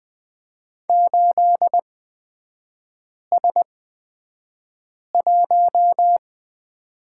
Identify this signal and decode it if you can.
Morse code, heard as a keyed tone, about 20 wpm, spaced as for 5 wpm, 690 Hz. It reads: 8S1
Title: 236 Lesson 38e - Letter q - Three Characters 20wpm